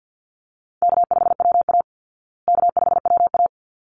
Morse code, heard as a keyed tone, about 49 wpm, 700 Hz. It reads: K5WA K5WA
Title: Easy Call Signs Repeated 2x - 50wpm